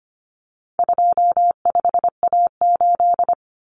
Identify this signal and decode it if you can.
Morse code, heard as a keyed tone, about 25 wpm, 700 Hz. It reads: 25A8
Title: Common Cut Numbers - 4-Digit 25wpm